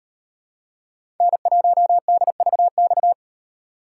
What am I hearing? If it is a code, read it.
Morse code, heard as a keyed tone, about 36 wpm, 700 Hz. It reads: N1DVX